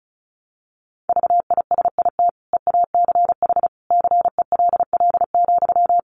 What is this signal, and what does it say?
Morse code, heard as a keyed tone, about 34 wpm, 700 Hz. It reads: VISIT EACH CELL,
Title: Sentences from Top 1100 Words 35wpm